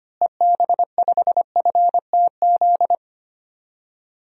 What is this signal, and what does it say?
Morse code, heard as a keyed tone, about 25 wpm, 700 Hz. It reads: EB5FTZ
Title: DXCC-100 Call Signs - 6 Characters - Repeated 2x - 25wpm